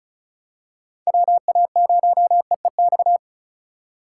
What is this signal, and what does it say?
Morse code, heard as a keyed tone, about 36 wpm, 700 Hz. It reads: WA0EEX
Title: USA Call Signs - 6 Characters - 2x Character Spacing - 35wpm